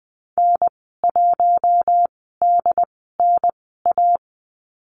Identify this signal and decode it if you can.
Morse code, heard as a keyed tone, about 20 wpm, spaced as for 14 wpm, 700 Hz. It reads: N1DNA
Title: 069 Lesson 12g - Letter d - Call Signs 20wpm